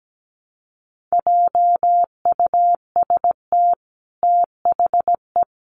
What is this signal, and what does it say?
Morse code, heard as a keyed tone, about 17 wpm, 700 Hz. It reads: JUST THE
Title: Sentences from Top 1300 Words 17wpm